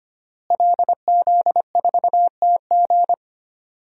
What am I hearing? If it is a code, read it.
Morse code, heard as a keyed tone, about 25 wpm, 700 Hz. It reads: LZ4TG
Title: DXCC-100 Call Signs - 5 Characters - Repeated 2x - 25wpm